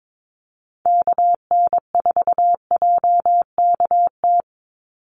Morse code, heard as a keyed tone, about 22 wpm, 700 Hz. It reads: KN4JKT
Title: USA Call Signs 22wpm